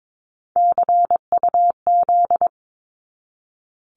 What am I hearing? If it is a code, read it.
Morse code, heard as a keyed tone, about 22 wpm, 700 Hz. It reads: CUZ
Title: Abbreviations 22wpm